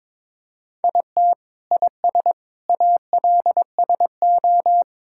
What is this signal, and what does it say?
Morse code, heard as a keyed tone, about 22 wpm, 700 Hz. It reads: IT IS ALSO
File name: practice / 22wpm / Sentences with 8-Word Phrases - 22wpm